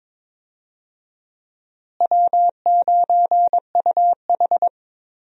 Morse code, heard as a keyed tone, about 22 wpm, 700 Hz. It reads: W9UH
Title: USA Call Signs - 4 Characters - Repeated 2x - 22wpm